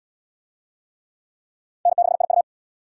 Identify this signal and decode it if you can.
Morse code, heard as a keyed tone, about 65 wpm, 700 Hz. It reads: N6IU